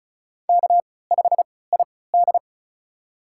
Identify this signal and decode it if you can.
Morse code, heard as a keyed tone, about 34 wpm, spaced as for 20 wpm, 700 Hz. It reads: K5ID
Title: USA Call Signs - 4 Characters - 3x Character Spacing - 35wpm